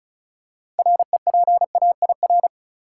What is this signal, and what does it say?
Morse code, heard as a keyed tone, about 36 wpm, 700 Hz. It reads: REPAIR
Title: Six Letter Words - 35wpm